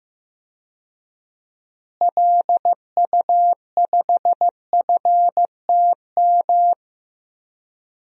Morse code, heard as a keyed tone, about 16 wpm, 700 Hz. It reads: LU5FTM
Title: DXCC-100 Call Signs - 6 Characters - Repeated 2x - 15wpm